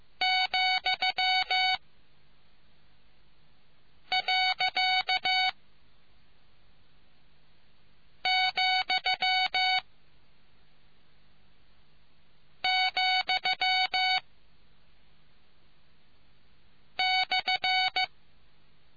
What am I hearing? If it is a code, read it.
Morse code, heard as a keyed tone, about 15 wpm, spaced as for 3 wpm, 750 Hz. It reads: ,.,,/